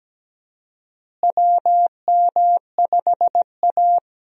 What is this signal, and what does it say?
Morse code, heard as a keyed tone, about 17 wpm, 700 Hz. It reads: WM5A
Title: USA Call Signs - 4 Characters - Repeated 3x - 17wpm